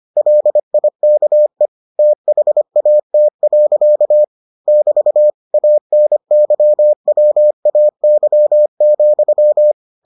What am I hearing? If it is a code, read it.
Morse code, heard as a keyed tone, about 25 wpm, 600 Hz. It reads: LIKE THAT. <BT> ANYWAY,